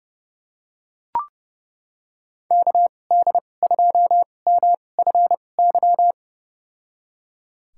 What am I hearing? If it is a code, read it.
Morse code, heard as a keyed tone, about 32 wpm, spaced as for 22 wpm, 700 Hz. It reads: KD2MFY
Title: USA Call Signs - 6 Characters - 2x Character Spacing - 30wpm